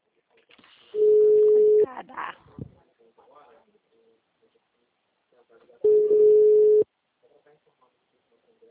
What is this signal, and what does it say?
Ringback tone (Europe, 425 Hz), with speech.